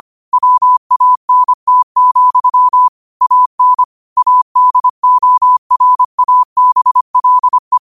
Morse code, heard as a keyed tone, about 25 wpm, 1000 Hz. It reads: WANT, AN ADORABLE